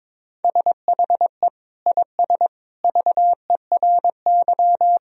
Morse code, heard as a keyed tone, about 23 wpm, 700 Hz. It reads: SHE IS VERY
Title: Sentences from Top 1000 Words - Review Part 2 - 22wpm